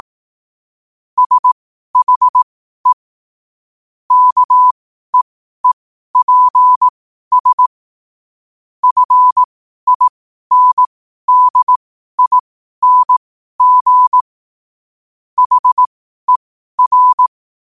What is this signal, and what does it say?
Morse code, heard as a keyed tone, about 18 wpm, spaced as for 12 wpm, 1000 Hz. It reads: SHE KEEPS FINDING HER
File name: qotdfiles / 12wpm